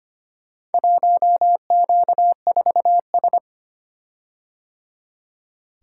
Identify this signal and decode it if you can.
Morse code, heard as a keyed tone, about 25 wpm, 700 Hz. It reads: 1Q4S
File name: practice / 25wpm / Four Letter-Number Combinations - 25wpm